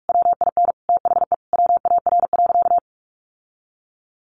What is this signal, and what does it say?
Morse code, heard as a keyed tone, about 44 wpm, 700 Hz. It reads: WIN THE WAR.
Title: Sentences from Top 1500 Words 45wpm